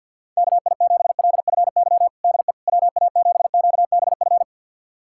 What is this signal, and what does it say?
Morse code, heard as a keyed tone, about 52 wpm, 700 Hz. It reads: KI7LFY DE WA7QBF
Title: USA Callsign DE Callsign 50wpm